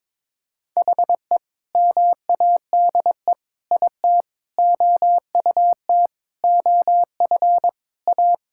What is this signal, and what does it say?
Morse code, heard as a keyed tone, about 22 wpm, 700 Hz. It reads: HE MADE IT OUT OF A
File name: practice / 22wpm / Sentences with 3-Word Phrases - 22wpm